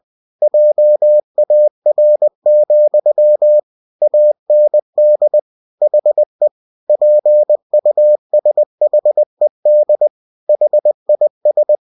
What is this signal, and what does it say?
Morse code, heard as a keyed tone, about 20 wpm, 600 Hz. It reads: JAR, AND HE PUSHED HIS
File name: cw-books / winnie-the-pooh / winnie-20wpm-page-0040